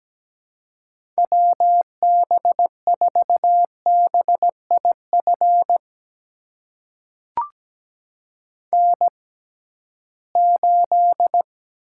Morse code, heard as a keyed tone, about 17 wpm, spaced as for 13 wpm, 700 Hz. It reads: WB4BIF N8
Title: USA Call Signs - 6x Character Spacing - 17wpm